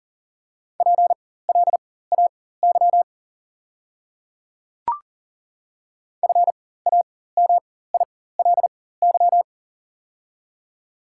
Morse code, heard as a keyed tone, about 40 wpm, spaced as for 19 wpm, 690 Hz. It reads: PLAY FAMILY
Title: 4x Character Spacing - Top 500 Words - 40wpm